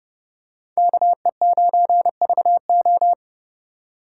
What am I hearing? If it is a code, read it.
Morse code, heard as a keyed tone, about 30 wpm, 700 Hz. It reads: KE9VO